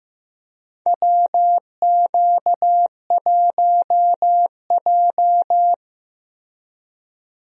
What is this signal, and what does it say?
Morse code, heard as a keyed tone, about 15 wpm, 700 Hz. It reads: WQ1J
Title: USA Call Signs - 4 Characters - Repeated 3x - 15wpm